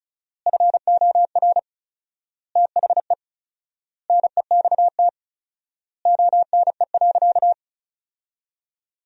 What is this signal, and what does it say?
Morse code, heard as a keyed tone, about 36 wpm, 700 Hz. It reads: FOR THE NEXT ONE.
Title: 4x Word-Spacing - Sentences from Top 400 Words - 35wpm